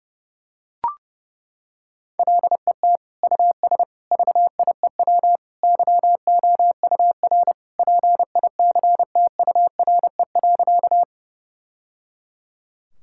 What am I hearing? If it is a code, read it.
Morse code, heard as a keyed tone, about 29 wpm, 700 Hz. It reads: LET US VIEW YOUR PICTURE.
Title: Sentences from Top 1000 Words - Review Part 6 - 30wpm